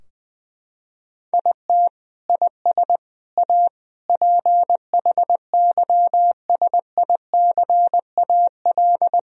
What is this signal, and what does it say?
Morse code, heard as a keyed tone, about 20 wpm, 700 Hz. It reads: IT IS A PHYSICAL
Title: Sentences from Top 1000 Words - Review Part 4 - 20wpm